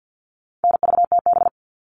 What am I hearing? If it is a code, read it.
Morse code, heard as a keyed tone, about 48 wpm, 700 Hz. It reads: NVTB
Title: All Cut Numbers - 4-Digit 50wpm